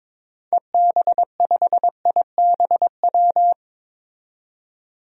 Morse code, heard as a keyed tone, about 22 wpm, 700 Hz. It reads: EB5IBW
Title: DXCC-100 Call Signs - 6 Characters - 22wpm